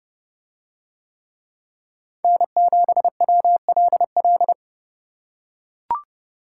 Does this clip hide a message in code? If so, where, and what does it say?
Yes: Morse code from 2.2 to 4.5 s, heard as a keyed tone, about 28 wpm, 700 Hz. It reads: N7WLL